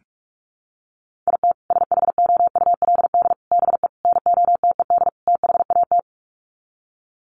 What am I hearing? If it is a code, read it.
Morse code, heard as a keyed tone, about 44 wpm, 700 Hz. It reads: IT SHOULD BE NOTED THAT
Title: Sets of 5 Words 45wpm